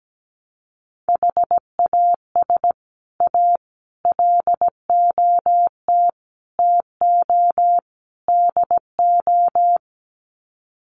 Morse code, heard as a keyed tone, about 17 wpm, 700 Hz. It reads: HAS A LOT TO DO
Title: Sets of 5 Words Encore2 17wpm